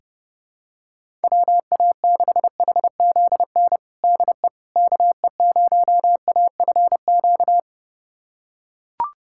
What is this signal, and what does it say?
Morse code, heard as a keyed tone, about 32 wpm, 700 Hz. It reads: WA6HZN DE KE0AFQ
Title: USA Callsign DE Callsign - 6 Characters - 30wpm